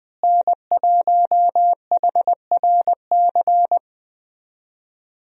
Morse code, heard as a keyed tone, about 20 wpm, 700 Hz. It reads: N1HRC
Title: USA Call Signs - 5 Characters - Repeated 3x - 20wpm